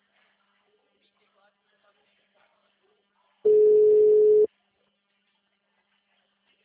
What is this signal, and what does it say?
Ringback tone (Europe, 425 Hz), with speech.